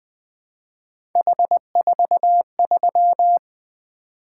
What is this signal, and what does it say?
Morse code, heard as a keyed tone, about 23 wpm, 700 Hz. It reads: H43
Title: Speed-Racing - Three Letter-Number Combinations - 20wpm